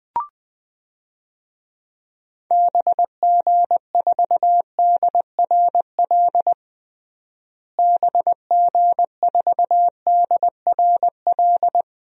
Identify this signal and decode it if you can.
Morse code, heard as a keyed tone, about 20 wpm, 700 Hz. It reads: BG4DRL BG4DRL
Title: DXCC-100 Call Signs - 6 Characters - Repeated 3x - 20wpm